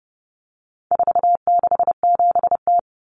Morse code, heard as a keyed tone, about 30 wpm, 700 Hz. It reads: 467T
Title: Common Cut Numbers - 4-Digit 30wpm